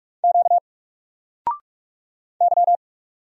Morse code, heard as a keyed tone, about 44 wpm, 700 Hz. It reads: Q Y
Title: Single Letter-Number 45wpm